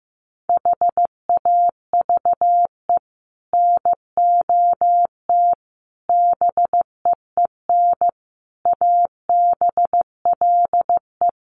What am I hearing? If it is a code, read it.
Morse code, heard as a keyed tone, about 15 wpm, 700 Hz. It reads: HAVE NOT BEEN ABLE
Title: Sets of 5 Words 15wpm